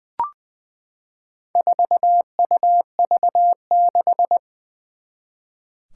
Morse code, heard as a keyed tone, about 20 wpm, 700 Hz. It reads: 4UV6
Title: Four Challenging Letter-Number Combinations 20wpm